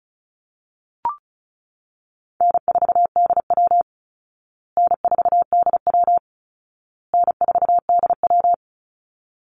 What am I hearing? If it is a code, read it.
Morse code, heard as a keyed tone, about 34 wpm, 700 Hz. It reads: N4DW N4DW N4DW
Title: Call Signs Repeated 2x - 35wpm